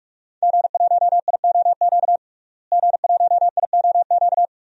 Morse code, heard as a keyed tone, about 46 wpm, 700 Hz. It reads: G1IOQ G1IOQ
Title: DXCC-100 Call Signs - 5 Characters - Repeated 3x - 45wpm